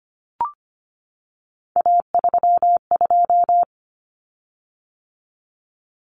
Morse code, heard as a keyed tone, about 25 wpm, 700 Hz. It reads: A32
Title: Common Cut Numbers - 3-Digit 25wpm